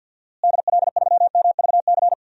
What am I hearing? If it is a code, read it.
Morse code, heard as a keyed tone, about 49 wpm, 700 Hz. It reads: DL3MVC
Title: DXCC-100 Call Signs - 6 Characters - Repeated 3x - 50wpm